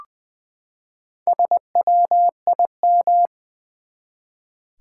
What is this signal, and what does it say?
Morse code, heard as a keyed tone, about 20 wpm, 700 Hz. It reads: SWIM
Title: Four Letter Words - 20wpm